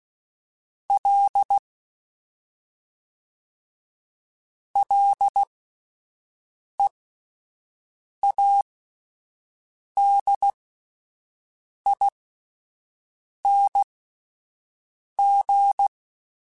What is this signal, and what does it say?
Morse code, heard as a keyed tone, about 16 wpm, spaced as for 5 wpm, 800 Hz. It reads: L LEADING
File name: Ls / Single L m3s / Single L Prefixe 3x2 Twice @5